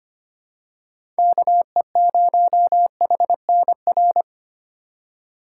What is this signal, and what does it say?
Morse code, heard as a keyed tone, about 25 wpm, 700 Hz. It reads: KE0HNR
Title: USA Call Signs - 6 Characters - 4x Character Spacing - 25wpm